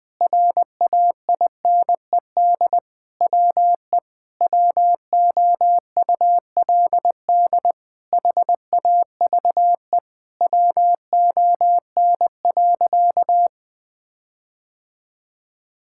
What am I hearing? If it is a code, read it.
Morse code, heard as a keyed tone, about 19 wpm, 700 Hz. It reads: RAINED WE WOULD HAVE WON.
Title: Sentences with 7-Word Phrases - 20wpm